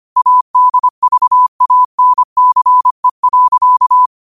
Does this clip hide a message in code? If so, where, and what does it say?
Yes: Morse code from 0.2 to 4.1 s, heard as a keyed tone, about 25 wpm, 1000 Hz. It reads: ADVANCE.